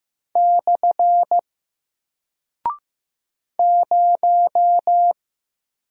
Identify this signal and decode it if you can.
Morse code, heard as a keyed tone, about 15 wpm, 700 Hz. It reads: / 0
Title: Single Character (Letter:Number:Punctuation:Prosign) Alt 15wpm